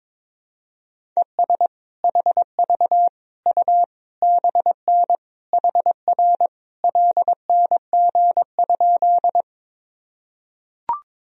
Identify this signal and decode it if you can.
Morse code, heard as a keyed tone, about 22 wpm, 700 Hz. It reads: ES HV U BN HR LNG?
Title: Sentences from Top 200 Words and Abbreviations 22wpm